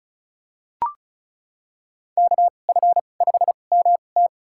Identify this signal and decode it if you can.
Morse code, heard as a keyed tone, about 36 wpm, spaced as for 25 wpm, 700 Hz. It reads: KF5MT
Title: USA Call Signs - 5 Characters - 2x Character Spacing - 35wpm